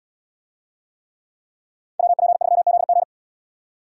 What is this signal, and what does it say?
Morse code, heard as a keyed tone, about 77 wpm, 700 Hz. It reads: WP3ZG